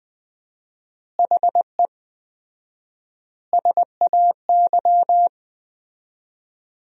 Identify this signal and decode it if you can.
Morse code, heard as a keyed tone, about 20 wpm, 700 Hz. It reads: HE SAY
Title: 4x Word-Spacing - Sentences from Top 300 Words - 20wpm